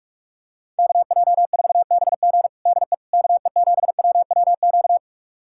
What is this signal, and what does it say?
Morse code, heard as a keyed tone, about 46 wpm, 700 Hz. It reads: KJ4BG DE KE7WWQ
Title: USA Callsign DE Callsign 45wpm